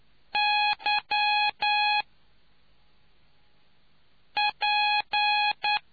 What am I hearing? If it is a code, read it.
Morse code, heard as a keyed tone, about 9 wpm, 830 Hz. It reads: Y P